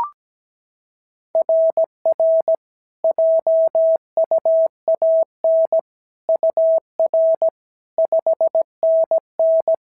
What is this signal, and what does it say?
Morse code, heard as a keyed tone, about 16 wpm, 650 Hz. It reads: RR JUAN UR 5NN
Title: SOTA QSO Practice 17wpm